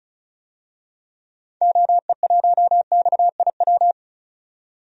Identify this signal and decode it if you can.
Morse code, heard as a keyed tone, about 34 wpm, 700 Hz. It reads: OE1XIW
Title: DXCC-100 Call Signs - 6 Characters - 35wpm